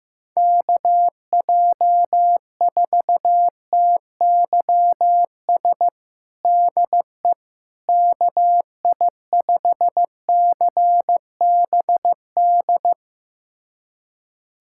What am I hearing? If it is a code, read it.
Morse code, heard as a keyed tone, about 15 wpm, 700 Hz. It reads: KJ4TYS DE KI5CBD
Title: USA Callsign DE Callsign - 6 Characters - 15wpm